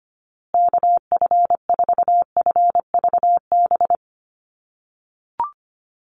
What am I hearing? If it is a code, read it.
Morse code, heard as a keyed tone, about 25 wpm, 700 Hz. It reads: KF4FVB
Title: USA Call Signs Repeated 3x 25wpm